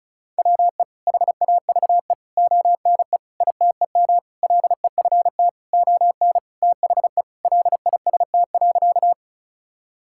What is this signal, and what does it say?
Morse code, heard as a keyed tone, about 36 wpm, 700 Hz. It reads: WE HAVE ONE ITEM LEFT ON THE LIST.
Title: Sentences from Top 1000 Words - Review Part 6 - 35wpm